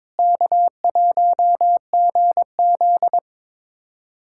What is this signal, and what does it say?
Morse code, heard as a keyed tone, about 22 wpm, 700 Hz. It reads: K1GZ
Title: USA Callsign DE Callsign - 4 Characters - 22wpm